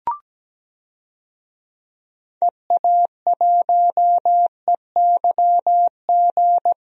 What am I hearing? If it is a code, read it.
Morse code, heard as a keyed tone, about 17 wpm, 700 Hz. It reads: EA1EYG
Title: DXCC-100 Call Signs - 6 Characters - 17wpm